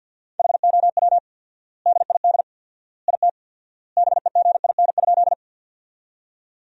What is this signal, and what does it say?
Morse code, heard as a keyed tone, about 50 wpm, 700 Hz. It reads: HOW DID IT BEGIN?
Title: 4x Word-Spacing - Sentences from Top 300 Words - 50wpm